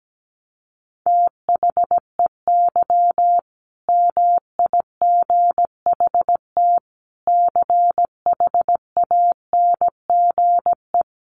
Morse code, heard as a keyed tone, about 17 wpm, 700 Hz. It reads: THEY MIGHT CHANGE
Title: A Sound Of Thunder 17wpm